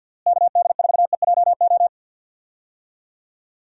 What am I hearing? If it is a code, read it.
Morse code, heard as a keyed tone, about 51 wpm, 700 Hz. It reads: KD4EJO
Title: USA Call Signs - 6 Characters - Repeated 3x - 50wpm